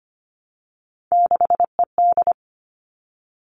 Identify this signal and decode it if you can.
Morse code, heard as a keyed tone, about 25 wpm, 700 Hz. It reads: 6ED
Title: All Cut Numbers - 3-Digit 25wpm